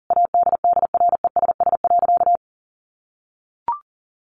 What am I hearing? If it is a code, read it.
Morse code, heard as a keyed tone, about 40 wpm, 700 Hz. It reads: ADDRESS.
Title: Sentences from Top 1100 Words 40wpm